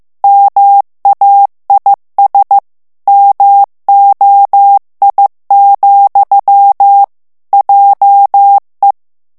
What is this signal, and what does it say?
Morse code, heard as a keyed tone, about 15 wpm, 800 Hz. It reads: MAIS MOI, JE
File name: FRANCAIS TEXTE CLAIR 2 VITESSE 14 MOTS MINUTES